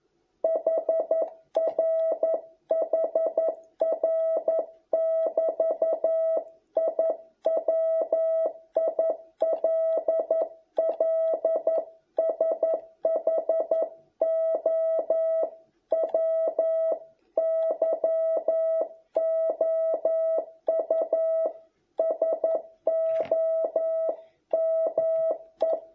Morse code, heard as a keyed tone, about 11 wpm, 650 Hz. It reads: HR HR <BT> I WILL SHOW YOU SOME